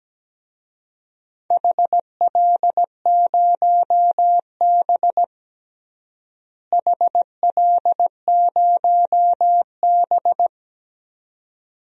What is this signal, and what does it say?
Morse code, heard as a keyed tone, about 17 wpm, 700 Hz. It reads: HL0B HL0B
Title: DXCC-100 Call Signs - 4 Characters - Repeated 2x - 17wpm